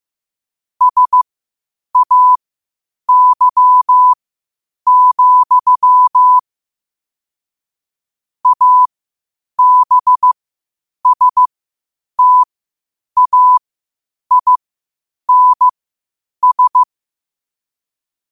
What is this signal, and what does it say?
Morse code, heard as a keyed tone, about 15 wpm, spaced as for 8 wpm, 1000 Hz. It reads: SAY, ABSTAINS
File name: qotdfiles / 8wpm